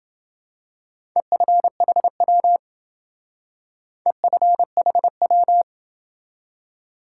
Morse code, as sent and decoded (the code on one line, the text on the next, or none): . ..-. .... .-- / . ..-. .... .--
EFHW EFHW